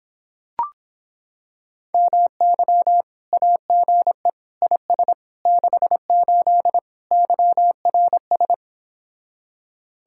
Morse code, heard as a keyed tone, about 26 wpm, 700 Hz. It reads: MY AGE IS 68 YRS